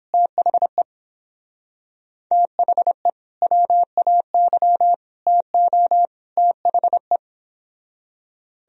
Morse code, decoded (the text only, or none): THE THE WAY TO THE